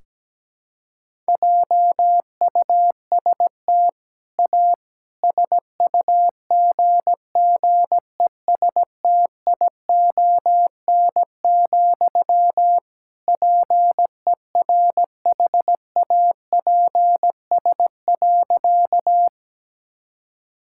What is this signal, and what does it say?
Morse code, heard as a keyed tone, about 17 wpm, 700 Hz. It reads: JUST A SUGGESTION, PERHAPS.
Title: Sentences from Top 1600 Words 17wpm